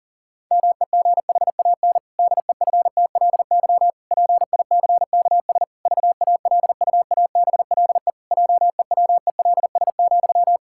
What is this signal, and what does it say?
Morse code, heard as a keyed tone, about 40 wpm, 700 Hz. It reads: MEGHAN DEFTLY PICKS VALUABLE JEWELS,
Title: Pangram Challenge v2 - 40wpm